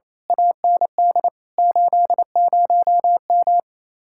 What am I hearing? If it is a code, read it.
Morse code, heard as a keyed tone, about 29 wpm, 700 Hz. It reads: AND 80M
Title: Speed-Racing Common QSO Phrases - 28wpm